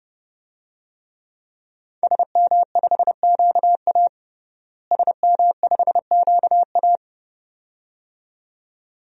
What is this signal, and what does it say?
Morse code, heard as a keyed tone, about 30 wpm, 700 Hz. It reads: SM5QA SM5QA
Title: DXCC-100 Call Signs - 5 Characters - Repeated 2x - 30wpm